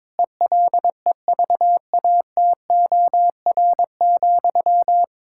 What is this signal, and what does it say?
Morse code, heard as a keyed tone, about 22 wpm, 700 Hz. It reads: ELEVATOR,
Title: Sentences from Top 1600 Words 22wpm